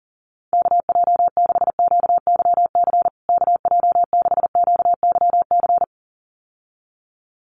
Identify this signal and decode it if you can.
Morse code, heard as a keyed tone, about 40 wpm, 700 Hz. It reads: KJ6QYC KJ6QYC
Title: USA Call Signs Repeated 2x 40wpm